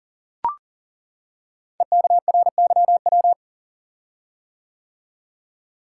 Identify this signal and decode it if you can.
Morse code, heard as a keyed tone, about 39 wpm, 690 Hz. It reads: EKRYW
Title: Code Groups - 5 Letters - 40wpm